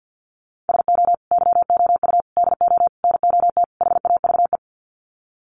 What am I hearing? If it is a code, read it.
Morse code, heard as a keyed tone, about 49 wpm, 700 Hz. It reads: SO YOU DO NOT HAVE
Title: Sets of 5 Words Encore 50wpm